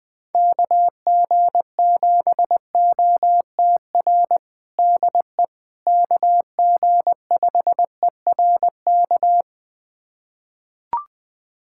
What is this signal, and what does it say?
Morse code, heard as a keyed tone, about 21 wpm, 700 Hz. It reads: KG7OTR DE KG5ERK
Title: USA Callsign DE Callsign - 6 Characters - 20wpm